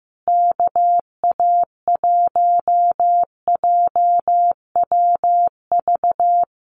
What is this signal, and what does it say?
Morse code, heard as a keyed tone, about 15 wpm, 700 Hz. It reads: KA1JWV
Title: Common QSO Phrases Encore 15wpm